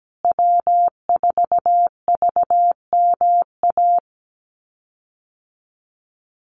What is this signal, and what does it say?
Morse code, heard as a keyed tone, about 17 wpm, 700 Hz. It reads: W4VMA
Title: USA Call Signs Repeated 2x 17wpm